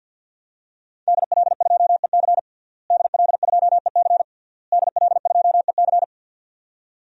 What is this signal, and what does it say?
Morse code, heard as a keyed tone, about 52 wpm, 700 Hz. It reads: DL2EC DL2EC DL2EC
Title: DXCC-100 Call Signs - 5 Characters - Repeated 3x - 50wpm